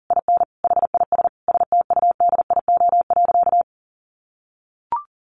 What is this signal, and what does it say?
Morse code, heard as a keyed tone, about 37 wpm, 700 Hz. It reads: IN HIS STUDIO.
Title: Sentences from Top 1200 Words 40wpm